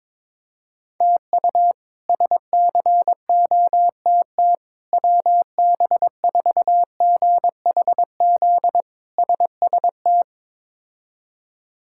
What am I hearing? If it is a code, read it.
Morse code, heard as a keyed tone, about 22 wpm, 700 Hz. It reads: TU SCOTT WB4GHZ SST
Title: K1USN SST QSO Practice - 22wpm